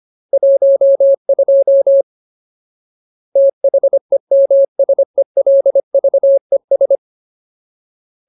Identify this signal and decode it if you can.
Morse code, heard as a keyed tone, about 25 wpm, 550 Hz. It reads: 12 THEMSELVES
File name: Ultimate Word List / 25WPM Ultimate